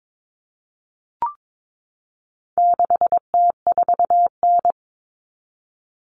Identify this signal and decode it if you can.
Morse code, heard as a keyed tone, about 22 wpm, 700 Hz. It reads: 6T4N
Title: Common Cut Numbers - 4-Digit 22wpm